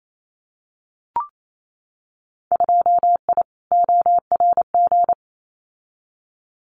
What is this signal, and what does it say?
Morse code, heard as a keyed tone, about 28 wpm, 700 Hz. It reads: 2I ORG
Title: Winter Field Day Exchanges Encore 28wpm